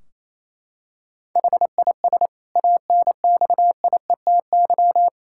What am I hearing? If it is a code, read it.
Morse code, heard as a keyed tone, about 28 wpm, 700 Hz. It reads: HIS ANXIETY
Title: Sentences from Top 1600 Words 28wpm